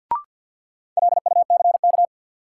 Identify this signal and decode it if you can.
Morse code, heard as a keyed tone, about 51 wpm, 700 Hz. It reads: LUCK
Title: Sentences from Top 1500 Words - New Word Drill Alt - 50wpm